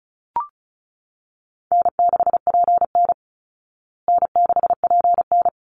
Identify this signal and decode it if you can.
Morse code, heard as a keyed tone, about 34 wpm, 700 Hz. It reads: N6PN N6PN
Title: Easy Call Signs Repeated 2x - 35wpm